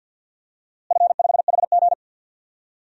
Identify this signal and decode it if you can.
Morse code, heard as a keyed tone, about 50 wpm, 700 Hz. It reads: F5HG
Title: DXCC-100 Call Signs - 4 Characters - Repeated 3x - 50wpm